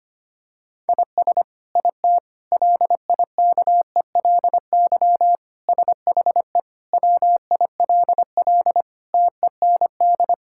Morse code, heard as a keyed tone, about 25 wpm, 700 Hz. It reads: IS IT LIKELY SHE WILL TEND
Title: Sentences from Top 1000 Words - Review Part 5 - 25wpm